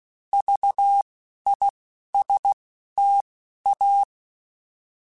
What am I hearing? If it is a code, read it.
Morse code, heard as a keyed tone, about 16 wpm, spaced as for 11 wpm, 800 Hz. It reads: VISTA
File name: Alphabetical / Alphabetical mp3s / Vs @10